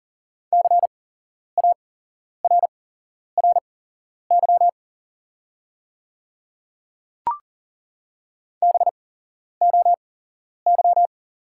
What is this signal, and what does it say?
Morse code, heard as a keyed tone, about 40 wpm, spaced as for 11 wpm, 700 Hz. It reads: CARRY BOY
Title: 8x Character Spacing - Top 300 Words - 40wpm